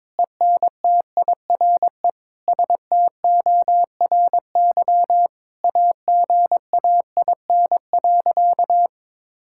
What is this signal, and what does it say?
Morse code, heard as a keyed tone, about 21 wpm, 700 Hz. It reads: ENTIRE STORY AGAIN.
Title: Sentences from Top 1000 Words - Review Part 5 - 22wpm